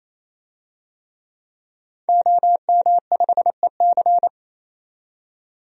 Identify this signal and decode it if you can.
Morse code, heard as a keyed tone, about 28 wpm, 700 Hz. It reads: OM5EC
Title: DXCC-100 Call Signs - 5 Characters - 28wpm